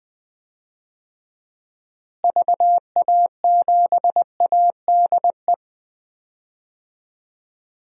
Morse code, heard as a keyed tone, about 20 wpm, 700 Hz. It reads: VA7ADE
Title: DXCC-100 Call Signs - 6 Characters - Repeated 2x - 20wpm